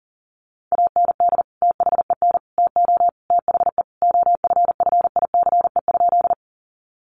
Morse code, heard as a keyed tone, about 40 wpm, 700 Hz. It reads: AND THEN TO THE OFFICE?
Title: Sentences from Top 1000 Words 40wpm